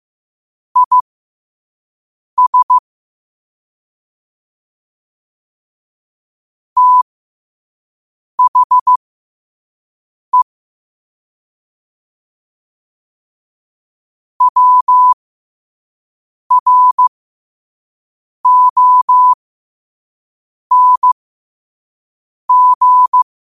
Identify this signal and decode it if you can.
Morse code, heard as a keyed tone, about 15 wpm, spaced as for 5 wpm, 1000 Hz. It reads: IS THE WRONG
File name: qotdfiles / 5wpm